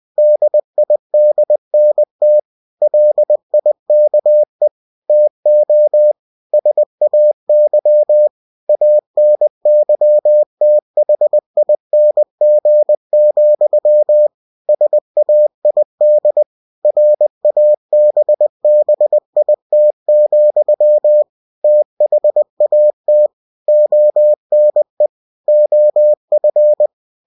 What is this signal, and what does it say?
Morse code, heard as a keyed tone, about 20 wpm, 600 Hz. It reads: DIDNT LIKE TO SAY ANYTHING, SAID RABBIT, THAT ONE OF
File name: cw-books / winnie-the-pooh / winnie-20wpm-page-0017